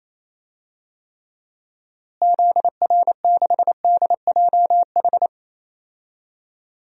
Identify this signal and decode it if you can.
Morse code, heard as a keyed tone, about 27 wpm, 700 Hz. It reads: ZR6DJH
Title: DXCC-100 Call Signs - 6 Characters - Repeated 2x - 28wpm